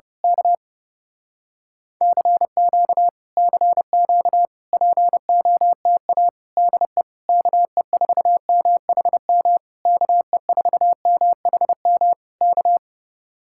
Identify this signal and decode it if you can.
Morse code, heard as a keyed tone, about 30 wpm, 700 Hz. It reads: K CQ CQ POTA DE KE4MHM KE4MHM K
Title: Speed-Racing Common QSO Phrases Encore - 30wpm